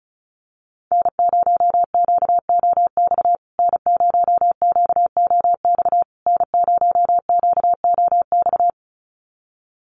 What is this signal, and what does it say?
Morse code, heard as a keyed tone, about 34 wpm, 700 Hz. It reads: N0QOX N0QOX N0QOX
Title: USA Call Signs Repeated 3x 35wpm